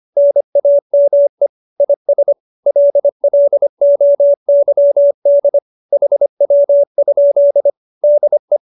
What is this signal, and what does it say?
Morse code, heard as a keyed tone, about 25 wpm, 570 Hz. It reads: NAME IS LLOYD HW? DE